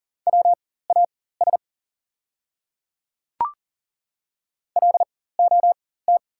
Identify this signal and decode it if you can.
Morse code, heard as a keyed tone, about 39 wpm, spaced as for 18 wpm, 700 Hz. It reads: WAS LOT
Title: 4x Character Spacing - Top 500 Words - 40wpm